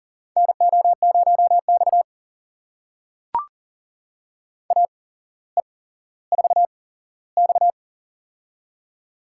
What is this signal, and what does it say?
Morse code, heard as a keyed tone, about 42 wpm, spaced as for 17 wpm, 700 Hz. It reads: NO0X AE4X
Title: USA Call Signs - 4 Characters - 8x Character Spacing - 40wpm